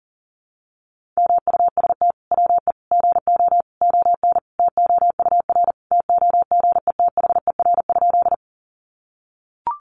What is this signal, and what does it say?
Morse code, heard as a keyed tone, about 40 wpm, 700 Hz. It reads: MUST WE GO ON TOUR TOGETHER?
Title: Sentences from Top 1400 Words 40wpm